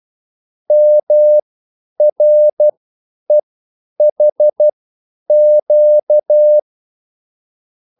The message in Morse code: MREHQ

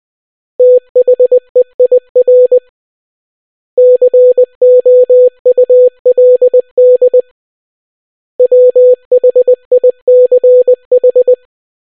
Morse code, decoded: THEIR COULD WHICH

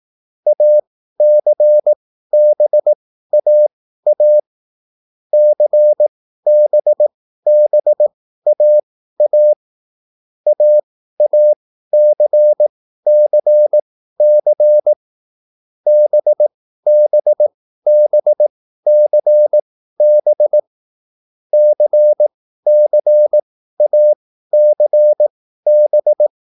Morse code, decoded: ACBAA CBBAA AACCC BBBCB CCACB

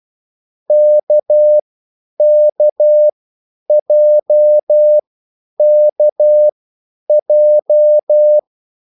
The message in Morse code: KKJKJ

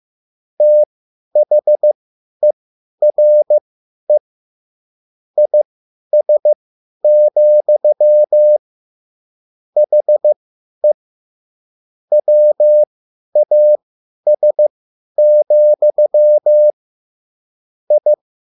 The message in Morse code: THERE IS, HE WAS, I

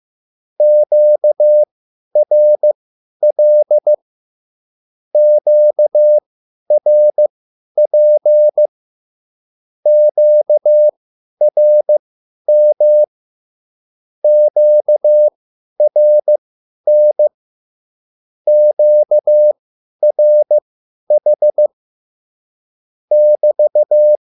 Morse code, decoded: QRL QRP QRM QRN QRH <BT>

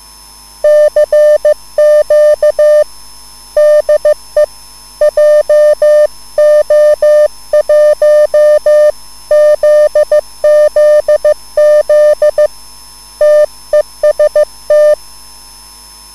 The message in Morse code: CQ DE JO1 ZZZ TEST

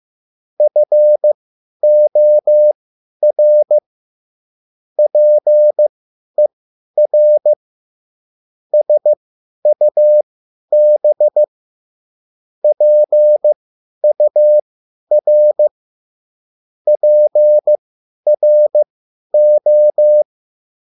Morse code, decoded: FOR PER SUB PUR PRO

